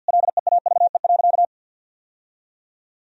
Morse code, heard as a keyed tone, about 53 wpm, 700 Hz. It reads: LEAVE.
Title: Sentences from Top 1000 Words - Review Part 1 - 50wpm